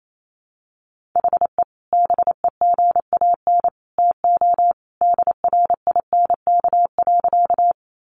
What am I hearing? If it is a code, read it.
Morse code, heard as a keyed tone, about 28 wpm, 700 Hz. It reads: HE BEGAN TO DRINK.
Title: Sentences from Top 1500 Words 28wpm